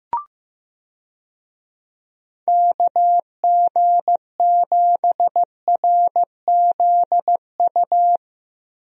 Morse code, heard as a keyed tone, about 16 wpm, 700 Hz. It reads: KG7RZU
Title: USA Call Signs - 6 Characters - Repeated 3x - 15wpm